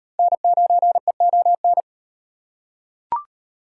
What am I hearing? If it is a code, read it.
Morse code, heard as a keyed tone, about 36 wpm, 700 Hz. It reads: N9EON